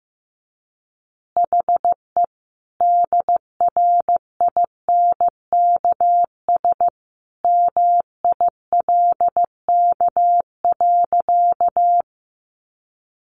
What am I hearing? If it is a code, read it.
Morse code, heard as a keyed tone, about 15 wpm, 700 Hz. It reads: HE DRINKS MILK.
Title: Easy Phrases 15wpm